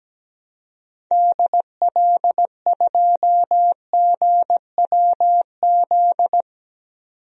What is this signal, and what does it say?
Morse code, heard as a keyed tone, about 17 wpm, 700 Hz. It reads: DL2GWZ
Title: DXCC-100 Call Signs - 6 Characters - 17wpm